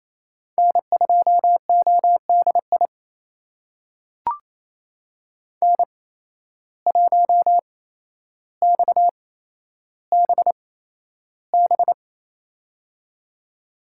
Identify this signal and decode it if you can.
Morse code, heard as a keyed tone, about 28 wpm, spaced as for 12 wpm, 700 Hz. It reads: N2ODI N1XBB
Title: USA Call Signs - 5 Characters - 8x Character Spacing - 28wpm